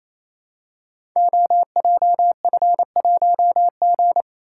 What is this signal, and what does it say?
Morse code, heard as a keyed tone, about 29 wpm, 700 Hz. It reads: OJF1G
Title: Code Groups - 5 Letter:Numbers - 28wpm